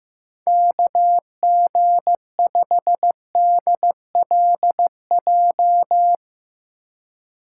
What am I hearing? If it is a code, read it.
Morse code, heard as a keyed tone, about 15 wpm, 700 Hz. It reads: KG5DLJ
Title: USA Call Signs - 6 Characters - 4x Character Spacing - 15wpm